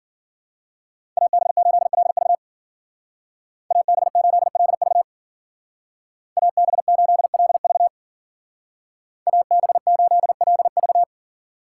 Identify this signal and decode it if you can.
Morse code, heard as a keyed tone, about 52 wpm, 700 Hz. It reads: AB8LV AB8LV AB8LV AB8LV